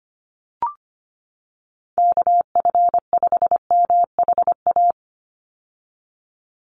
Morse code, heard as a keyed tone, about 25 wpm, 700 Hz. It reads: KF5MHA
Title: USA Call Signs 25wpm